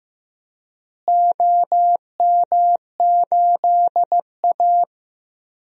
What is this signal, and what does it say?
Morse code, heard as a keyed tone, about 15 wpm, 700 Hz. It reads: OM8A
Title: DXCC-100 Call Signs - 4 Characters - Repeated 2x - 15wpm